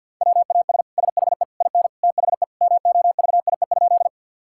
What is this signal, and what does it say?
Morse code, heard as a keyed tone, about 50 wpm, 700 Hz. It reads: WAS SHE IN THE MOVIE?
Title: Sentences from Top 1000 Words - Review Part 5 - 50wpm